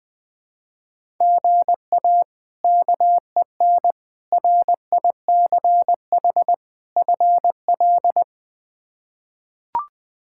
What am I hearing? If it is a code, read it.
Morse code, heard as a keyed tone, about 20 wpm, 700 Hz. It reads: GA KEN RICH FL
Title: K1USN SST QSO Practice - 20wpm